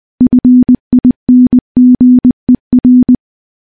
Morse code, heard as a keyed tone, about 20 wpm, 250 Hz. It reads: FINGER
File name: morse-training-20WPM